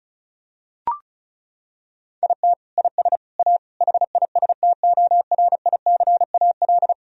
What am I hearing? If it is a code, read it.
Morse code, heard as a keyed tone, about 34 wpm, 700 Hz. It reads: IT IS A HISTORICAL
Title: Sentences from Top 1000 Words - Review Part 4 - 35wpm